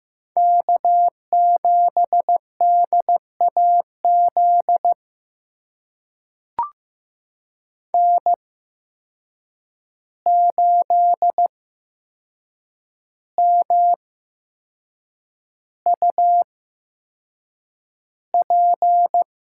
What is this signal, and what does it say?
Morse code, heard as a keyed tone, about 15 wpm, spaced as for 6 wpm, 700 Hz. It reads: K7DAZ N8MUP
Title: USA Call Signs - 5 Characters - 8x Character Spacing - 15wpm